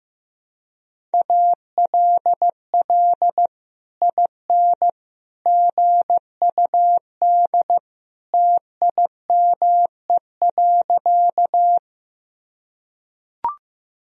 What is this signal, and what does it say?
Morse code, heard as a keyed tone, about 15 wpm, 700 Hz. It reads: ALL IN GUD TIME.
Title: Sentences from Top 200 Words and Abbreviations 15wpm